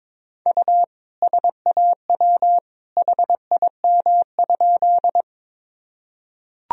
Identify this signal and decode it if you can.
Morse code, heard as a keyed tone, about 22 wpm, 700 Hz. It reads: U SAW HIM?
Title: Sentences from Top 300 Words and Abbreviations 22wpm